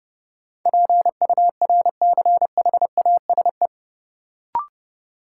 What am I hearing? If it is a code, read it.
Morse code, heard as a keyed tone, about 32 wpm, 700 Hz. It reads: PURCHASE
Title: Sentences from Top 800 Words - New Word Drill - 30wpm